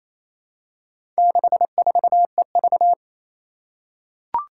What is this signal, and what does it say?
Morse code, heard as a keyed tone, about 28 wpm, 700 Hz. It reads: 64EV
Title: Four Challenging Letter-Number Combinations 28wpm